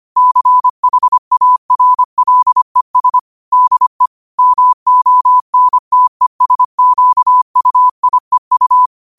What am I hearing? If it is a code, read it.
Morse code, heard as a keyed tone, about 25 wpm, 1000 Hz. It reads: CHARLES DE MONTESQUIEU
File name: 25wpm